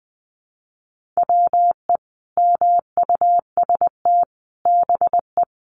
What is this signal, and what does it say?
Morse code, heard as a keyed tone, about 20 wpm, 700 Hz. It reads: WE MUST BE
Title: Sentences from Top 1400 Words 20wpm